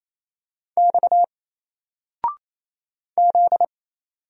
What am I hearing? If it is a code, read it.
Morse code, heard as a keyed tone, about 28 wpm, 700 Hz. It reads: X Z
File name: Single-Letter-Number-28wpm